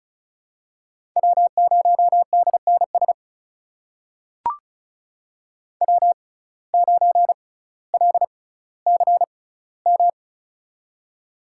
Morse code, heard as a keyed tone, about 36 wpm, spaced as for 18 wpm, 700 Hz. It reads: W0DNS W9LCM
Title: USA Call Signs - 5 Characters - 6x Character Spacing - 35wpm